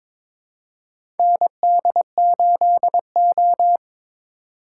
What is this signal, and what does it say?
Morse code, heard as a keyed tone, about 22 wpm, 700 Hz. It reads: ND8O
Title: USA Call Signs - 4 Characters - 3x Character Spacing - 22wpm